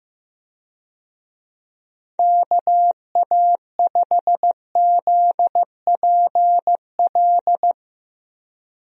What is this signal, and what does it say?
Morse code, heard as a keyed tone, about 15 wpm, 700 Hz. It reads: KA5ZPL
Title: USA Call Signs - 6 Characters - Repeated 2x - 15wpm